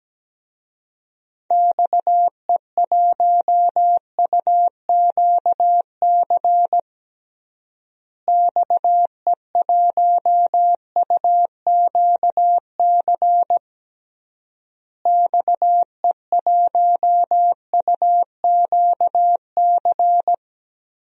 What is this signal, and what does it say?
Morse code, heard as a keyed tone, about 17 wpm, 700 Hz. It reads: XE1UQC XE1UQC XE1UQC
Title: DXCC-100 Call Signs - 6 Characters - Repeated 3x - 17wpm